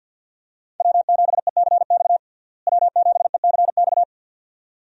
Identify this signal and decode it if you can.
Morse code, heard as a keyed tone, about 50 wpm, 700 Hz. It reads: W7ECX W7ECX
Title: USA Call Signs - 5 Characters - Repeated 2x - 50wpm